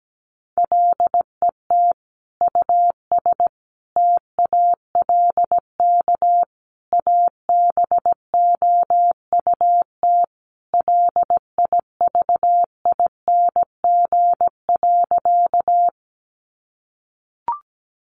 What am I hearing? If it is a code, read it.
Morse code, heard as a keyed tone, about 17 wpm, 700 Hz. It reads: LET US TALK ABOUT LIVING.
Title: Sentences from Top 1100 Words 17wpm